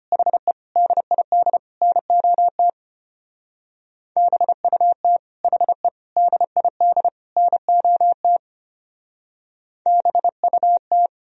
Morse code, heard as a keyed tone, about 31 wpm, 700 Hz. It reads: HE DID NOT BUT HE DID NOT BUT
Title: Speed-Racing - Sets of 4 Words 25wpm